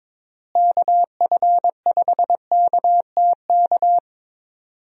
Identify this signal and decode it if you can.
Morse code, heard as a keyed tone, about 22 wpm, 700 Hz. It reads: KF5KTK
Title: USA Call Signs - 6 Characters - Repeated 3x - 22wpm